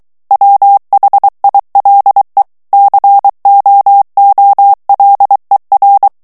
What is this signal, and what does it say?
Morse code, heard as a keyed tone, about 23 wpm, 800 Hz. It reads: WHILE COOLER
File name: morse mp3 / ANGLAIS texte clair 2 vitesse 22 mots minute